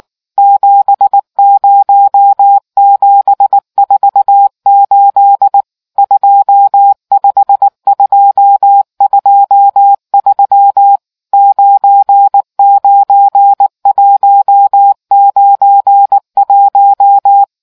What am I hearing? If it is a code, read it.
Morse code, heard as a keyed tone, about 19 wpm, 800 Hz. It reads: 70748 25223 99191